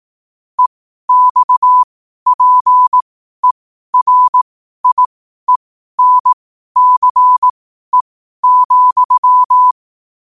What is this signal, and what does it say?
Morse code, heard as a keyed tone, about 18 wpm, spaced as for 12 wpm, 1000 Hz. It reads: EXPERIENCE,
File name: qotdfiles / 12wpm